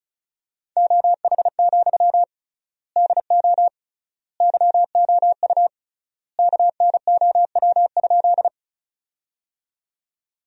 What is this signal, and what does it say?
Morse code, heard as a keyed tone, about 36 wpm, 700 Hz. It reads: OH, DO YOU KNOW?
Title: 3x Word-Spacing - Sentences from Top 500 Words - 35wpm